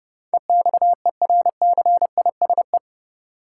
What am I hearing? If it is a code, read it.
Morse code, heard as a keyed tone, about 31 wpm, 700 Hz. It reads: EXERCISE
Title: Sentences from Top 1200 Words - New Word Drill Alt - 30wpm